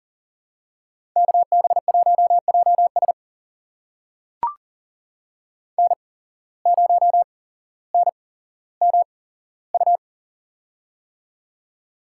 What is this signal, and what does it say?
Morse code, heard as a keyed tone, about 40 wpm, spaced as for 17 wpm, 700 Hz. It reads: KB1JS N0NMU
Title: USA Call Signs - 5 Characters - 8x Character Spacing - 40wpm